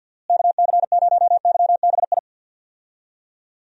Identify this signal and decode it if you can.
Morse code, heard as a keyed tone, about 47 wpm, 700 Hz. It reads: KC0YBI
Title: USA Call Signs - 6 Characters - Repeated 2x - 50wpm